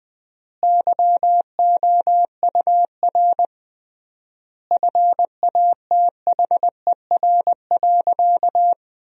Morse code, heard as a keyed tone, about 20 wpm, 700 Hz. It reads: YOUR FATHER.
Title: 3x Word-Spacing - Sentences from Top 300 Words - 20wpm